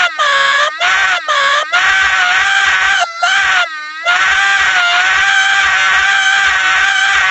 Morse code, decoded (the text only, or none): none